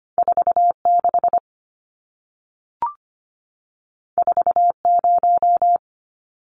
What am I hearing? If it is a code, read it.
Morse code, heard as a keyed tone, about 25 wpm, 700 Hz. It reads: 46 40
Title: Numbers - 2-Digit 25wpm